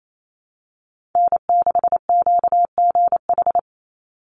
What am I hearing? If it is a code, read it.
Morse code, heard as a keyed tone, about 28 wpm, 700 Hz. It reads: N6QGH
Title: USA Call Signs 28wpm